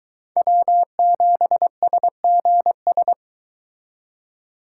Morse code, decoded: W7SGS

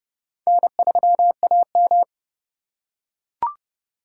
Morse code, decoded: N3AM

